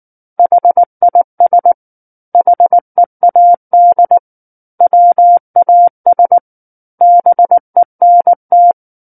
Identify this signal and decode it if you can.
Morse code, heard as a keyed tone, about 19 wpm, 700 Hz. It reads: HIS HEAD WAS BENT
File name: Book One / 10-Chapter 10 - 19 WPM